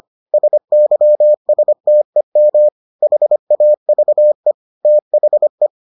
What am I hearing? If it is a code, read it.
Morse code, heard as a keyed tone, about 25 wpm, 600 Hz. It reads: SYSTEM HAVE THE